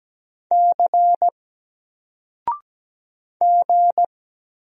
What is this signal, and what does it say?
Morse code, heard as a keyed tone, about 17 wpm, 700 Hz. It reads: C G
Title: Single Character (Letter:Number:Punctuation:Prosign) 17wpm